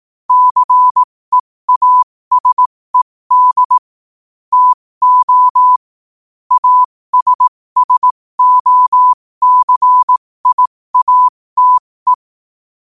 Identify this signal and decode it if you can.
Morse code, heard as a keyed tone, about 18 wpm, spaced as for 15 wpm, 1000 Hz. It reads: CEASED TO ASSOCIATE
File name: qotdfiles / 15wpm